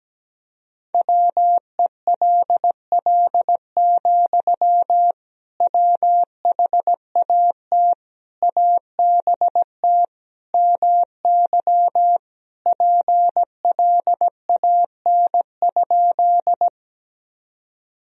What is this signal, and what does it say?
Morse code, heard as a keyed tone, about 17 wpm, 700 Hz. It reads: WELL, WHAT ABT MY PLAN?
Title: Sentences from Top 500 Words and Abbreviations 17wpm